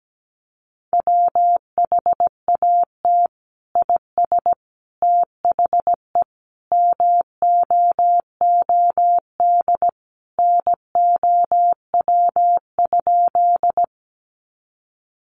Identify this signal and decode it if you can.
Morse code, heard as a keyed tone, about 17 wpm, 700 Hz. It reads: WHAT IS THE MOOD NOW?
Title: Sentences from Top 1100 Words 17wpm